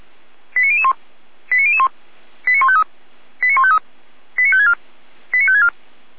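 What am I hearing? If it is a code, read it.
Five-tone selective call (ZVEI), heard as quick runs of five tones, tones of about 70 ms. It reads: 89001 89001 89153 89153 89564 89564
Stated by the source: ZVEI five-tone selective call received on SDR.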